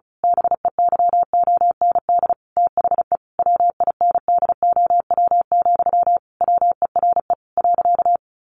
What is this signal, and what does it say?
Morse code, heard as a keyed tone, about 34 wpm, 700 Hz. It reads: BEYOND THE WINDOW, WERE .
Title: A Sound Of Thunder 35wpm